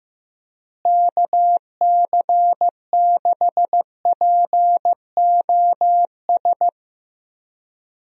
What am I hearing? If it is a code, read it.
Morse code, heard as a keyed tone, about 15 wpm, 700 Hz. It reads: KC6POS
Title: USA Call Signs - 6 Characters - Repeated 2x - 15wpm